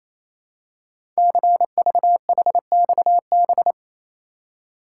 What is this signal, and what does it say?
Morse code, heard as a keyed tone, about 29 wpm, 700 Hz. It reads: CVHXB
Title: Code Groups - 5 Letters - 28wpm